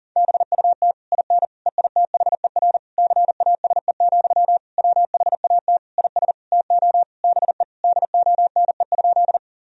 Morse code, heard as a keyed tone, about 37 wpm, 700 Hz. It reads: BUT IN EITHER CASE, WHAT IS TO BE DONE?
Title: Sentences from Top 1000 Words - Review Part 4 - 40wpm